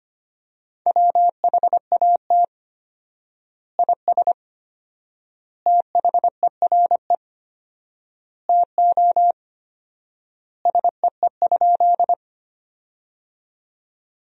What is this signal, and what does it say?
Morse code, heard as a keyed tone, about 25 wpm, 700 Hz. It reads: WHAT IS THERE TO SEE?
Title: 4x Word-Spacing - Sentences from Top 100 Words - 25wpm